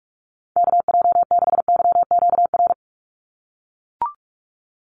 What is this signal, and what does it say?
Morse code, heard as a keyed tone, about 44 wpm, 700 Hz. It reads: KJ6YQR
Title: USA Call Signs 45wpm